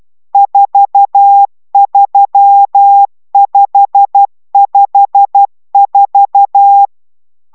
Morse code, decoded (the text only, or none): 43554